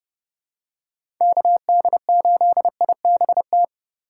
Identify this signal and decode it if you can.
Morse code, heard as a keyed tone, about 32 wpm, 700 Hz. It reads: KD8IBT
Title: USA Call Signs - 6 Characters - 2x Character Spacing - 30wpm